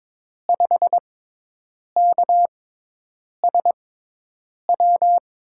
Morse code, heard as a keyed tone, about 22 wpm, spaced as for 8 wpm, 700 Hz. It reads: 5KSW
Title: USA Call Signs - 6 Characters - 6x Character Spacing - 22wpm